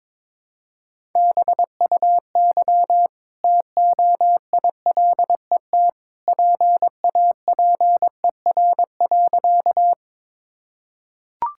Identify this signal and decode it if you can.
Morse code, heard as a keyed tone, about 22 wpm, 700 Hz. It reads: BUY TOILET PAPER.
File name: practice / 22wpm / Easy Sentences with Hints - 22wpm